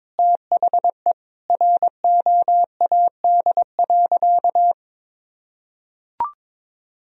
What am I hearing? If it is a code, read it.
Morse code, heard as a keyed tone, about 22 wpm, 700 Hz. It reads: THE ROAD.
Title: Sentences from Top 1000 Words - Review Part 2 - 22wpm